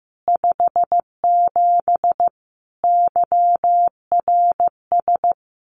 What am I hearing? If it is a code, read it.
Morse code, heard as a keyed tone, about 15 wpm, 700 Hz. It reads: 57 YRS
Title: Common QSO Phrases Encore 15wpm